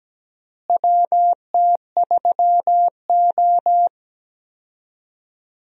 Morse code, heard as a keyed tone, about 17 wpm, 700 Hz. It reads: WT3O
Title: USA Call Signs - 4 Characters - Repeated 2x - 17wpm